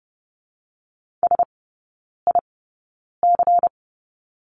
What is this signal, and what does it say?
Morse code, heard as a keyed tone, about 30 wpm, spaced as for 9 wpm, 700 Hz. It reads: SIC
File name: Lesson 18e - Letter m - Three Characters 30wpm